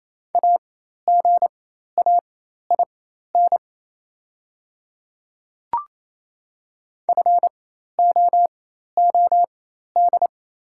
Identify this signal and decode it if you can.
Morse code, heard as a keyed tone, about 28 wpm, spaced as for 13 wpm, 700 Hz. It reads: AGAIN FOOD
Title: 4x Character Spacing - Top 300 Words - 28wpm